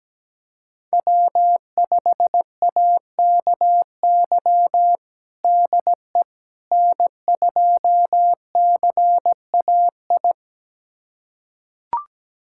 Morse code, heard as a keyed tone, about 17 wpm, 700 Hz. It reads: W5AKY DE N2CAI
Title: USA Callsign DE Callsign - 5 Characters - 17wpm